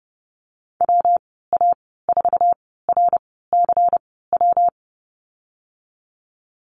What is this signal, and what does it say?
Morse code, heard as a keyed tone, about 30 wpm, spaced as for 17 wpm, 700 Hz. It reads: WA4RCW
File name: Lesson 19g - Letter w - Call Signs 30wpm